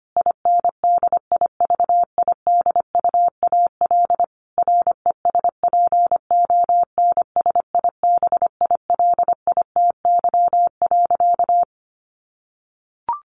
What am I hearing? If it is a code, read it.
Morse code, heard as a keyed tone, about 25 wpm, 700 Hz. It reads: INDIVIDUAL RESPONSIBILITY.
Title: Sentences from Top 1500 Words 25wpm